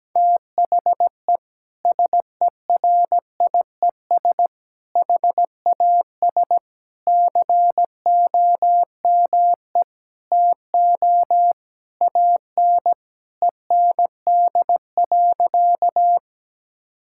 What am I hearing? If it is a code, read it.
Morse code, heard as a keyed tone, about 17 wpm, 700 Hz. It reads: THE SERIES HAS COME TO AN END.
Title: Sentences from Top 1000 Words - Review Part 5 - 17wpm